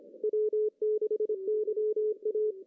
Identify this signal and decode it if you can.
Morse code, heard as a keyed tone, about 24 wpm, 440 Hz. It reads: W6YA
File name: CW / cqww-cw-k5tr